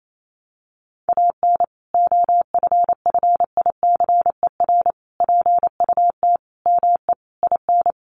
Morse code, heard as a keyed tone, about 28 wpm, 700 Hz. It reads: AN OFFICER PUT ME IN
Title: Sentences from Top 1200 Words 28wpm